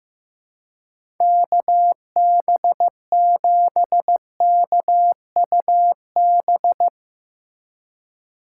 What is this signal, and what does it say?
Morse code, heard as a keyed tone, about 15 wpm, 700 Hz. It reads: KB7KUB
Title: USA Call Signs - 6 Characters - 3x Character Spacing - 15wpm